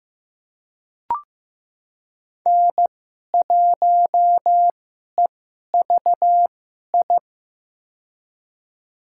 Morse code, heard as a keyed tone, about 15 wpm, spaced as for 11 wpm, 700 Hz. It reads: N1EVI
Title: USA Call Signs - 5 Characters - 2x Character Spacing - 15wpm